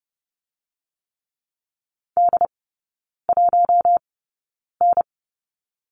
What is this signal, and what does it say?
Morse code, heard as a keyed tone, about 30 wpm, spaced as for 9 wpm, 700 Hz. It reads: D1N
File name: Lesson 16e - Letter c - Three Characters 30wpm